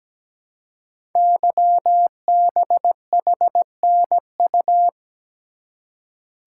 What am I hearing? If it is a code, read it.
Morse code, heard as a keyed tone, about 17 wpm, 700 Hz. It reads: YBHNU
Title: Code Groups - 5 Letter:Numbers - 17wpm